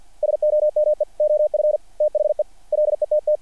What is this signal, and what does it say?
Morse code, heard as a keyed tone, about 48 wpm, 660 Hz. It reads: SOME OF THE LETT
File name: archived / CRSnet